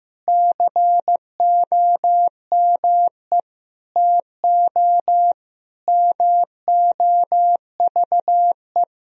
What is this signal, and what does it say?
Morse code, heard as a keyed tone, about 16 wpm, 700 Hz. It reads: COME TO MOVE
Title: Sentences from Top 1000 Words - Review Part 1 - 15wpm